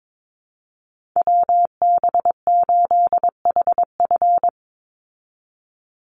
Morse code, heard as a keyed tone, about 22 wpm, 700 Hz. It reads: WB8HF
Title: Call Signs Repeated 3x - 22wpm